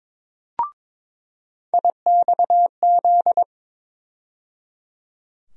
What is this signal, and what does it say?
Morse code, heard as a keyed tone, about 22 wpm, 700 Hz. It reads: IXZ
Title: USA Call Sign Postfixes - 3 Letters 22wpm